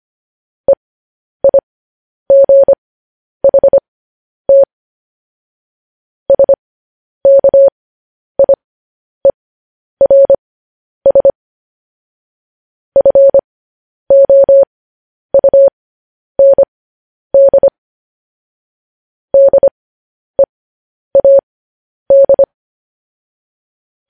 Morse code, heard as a keyed tone, about 25 wpm, spaced as for 10 wpm, 550 Hz. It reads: EIGHT SKIERS FOUND DEAD
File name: podcast / News headlines-1x-25x10